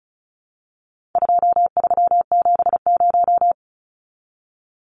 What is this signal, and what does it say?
Morse code, heard as a keyed tone, about 34 wpm, 700 Hz. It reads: 2370
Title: Numbers - 4-Digit 35wpm